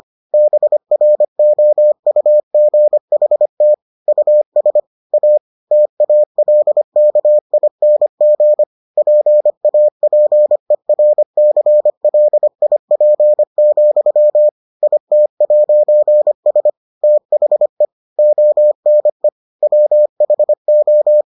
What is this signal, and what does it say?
Morse code, heard as a keyed tone, about 25 wpm, 600 Hz. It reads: BROUGHT US A TALKING PAPERCLIP, IT'S THE ONE WHO